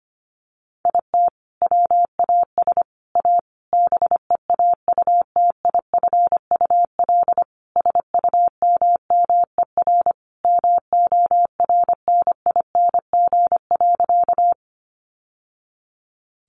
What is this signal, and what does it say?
Morse code, heard as a keyed tone, about 25 wpm, 690 Hz. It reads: IT WAS A BEAUTIFUL SUMMER MORNING.
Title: Sentences from Top 1100 Words 25wpm